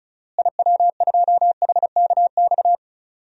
Morse code, heard as a keyed tone, about 36 wpm, 700 Hz. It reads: IW2HKX
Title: DXCC-100 Call Signs - 6 Characters - Repeated 3x - 35wpm